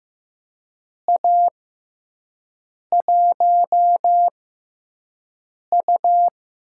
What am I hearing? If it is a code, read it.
Morse code, heard as a keyed tone, about 15 wpm, spaced as for 5 wpm, 700 Hz. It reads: A1U
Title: USA Call Signs - 6x Character Spacing - 15wpm